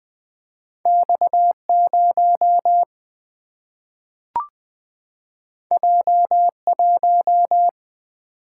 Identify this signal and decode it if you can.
Morse code, heard as a keyed tone, about 20 wpm, 700 Hz. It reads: X 0 J 1
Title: Single-Letter Single-Number 20wpm